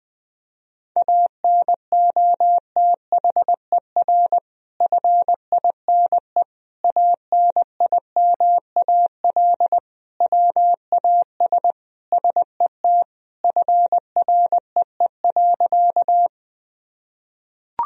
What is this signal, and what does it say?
Morse code, heard as a keyed tone, about 21 wpm, 700 Hz. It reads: ANOTHER FINE ANIMAL WAS SET FREE.
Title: Sentences from Top 1000 Words - Review Part 1 - 20wpm